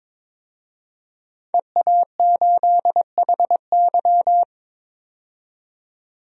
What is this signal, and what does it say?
Morse code, heard as a keyed tone, about 22 wpm, 700 Hz. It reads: EA8HY
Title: DXCC-100 Call Signs - 5 Characters - 22wpm